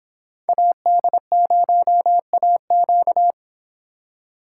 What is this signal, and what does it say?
Morse code, heard as a keyed tone, about 29 wpm, 700 Hz. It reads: AD0AQ